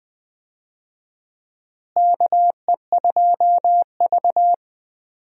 Morse code, heard as a keyed tone, about 20 wpm, 700 Hz. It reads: KE2V
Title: USA Call Signs - 4 Characters - Repeated 2x - 20wpm